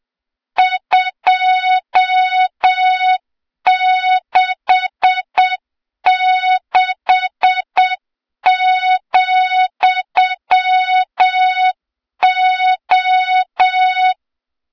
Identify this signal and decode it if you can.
Morse code, heard as a keyed tone, about 7 wpm, 3700 Hz, with noise. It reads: 266,O
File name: Morze-HA2MN / hang / 23 GYAKORLAT